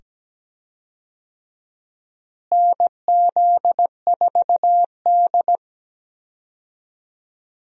Morse code, heard as a keyed tone, about 17 wpm, 700 Hz. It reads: NZ4D
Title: USA Call Signs - 4 Characters - Repeated 2x - 17wpm